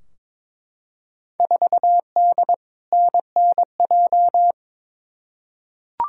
Morse code, heard as a keyed tone, about 22 wpm, 700 Hz. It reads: 4D NNJ
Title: Field Day Exchanges Encore - 22wpm